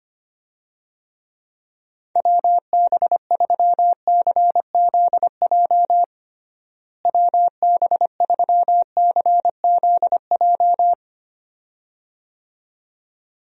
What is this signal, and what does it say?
Morse code, heard as a keyed tone, about 25 wpm, 700 Hz. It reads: WB3CZJ WB3CZJ
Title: USA Call Signs - 6 Characters - Repeated 2x - 25wpm